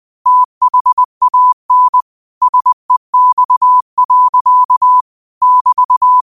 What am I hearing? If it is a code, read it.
Morse code, heard as a keyed tone, about 20 wpm, 1000 Hz. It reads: THAN SEX. <BT>
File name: qotdfiles / 20wpm